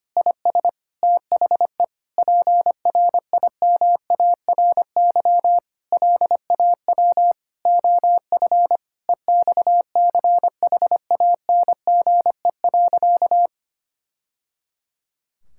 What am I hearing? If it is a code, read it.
Morse code, heard as a keyed tone, about 25 wpm, 700 Hz. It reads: IS THE PRIMARY LAW OF EXCHANGE.
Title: Sentences from Top 1000 Words - Review Part 6 - 25wpm